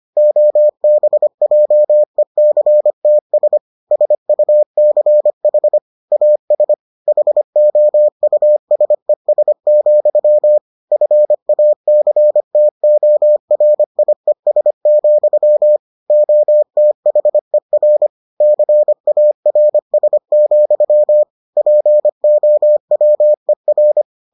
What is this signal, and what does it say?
Morse code, heard as a keyed tone, about 25 wpm, 600 Hz. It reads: OBJECTS SUCH AS HOUSES, FACTORIES, OTHER CARS, POWER